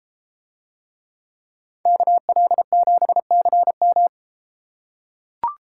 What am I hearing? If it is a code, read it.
Morse code, heard as a keyed tone, about 33 wpm, 700 Hz. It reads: KL7CM